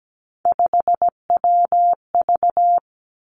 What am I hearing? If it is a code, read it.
Morse code, heard as a keyed tone, about 17 wpm, 700 Hz. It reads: 5WV
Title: Three Challenging Letter-Number Combinations 17wpm